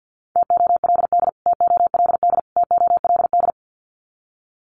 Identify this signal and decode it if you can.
Morse code, heard as a keyed tone, about 50 wpm, 700 Hz. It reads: TOLD TOLD TOLD
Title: Top 500 Words Repeated 50wpm